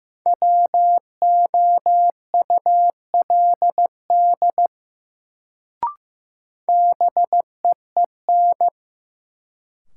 Morse code, heard as a keyed tone, about 15 wpm, 700 Hz. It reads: WOULD BEEN
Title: Sentences from Top 100 Words - New Word Drill - 15wpm